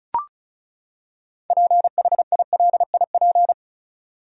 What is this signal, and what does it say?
Morse code, heard as a keyed tone, about 36 wpm, 700 Hz. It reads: PHILIP
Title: USA Common Names - 35wpm